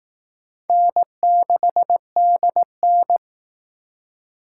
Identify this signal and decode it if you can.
Morse code, heard as a keyed tone, about 18 wpm, 700 Hz. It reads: N6DN
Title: Speed Racing - USA 1x2 Call Signs 15wpm